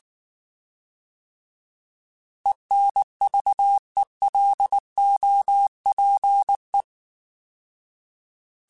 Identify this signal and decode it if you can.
Morse code, heard as a keyed tone, about 19 wpm, 800 Hz. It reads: ENVELOPE
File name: Alphabetical / Alphabetical mp3s / Es @18